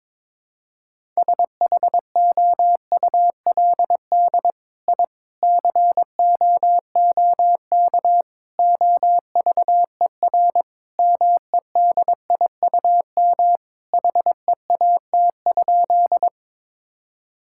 Morse code, heard as a keyed tone, about 22 wpm, 700 Hz. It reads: SHOULD I COOK OVER MEDIUM HEAT?
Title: Sentences from Top 1000 Words - Review Part 6 - 22wpm